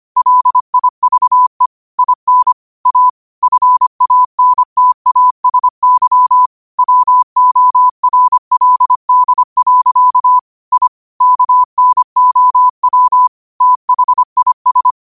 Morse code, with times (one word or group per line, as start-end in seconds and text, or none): 0.2-1.7 LIVE
2.0-2.5 IN
2.9-3.1 A
3.4-6.5 FANTASY
6.8-10.4 WORLD.
10.7-10.9 I
11.2-13.3 KNOW
13.6-14.9 THIS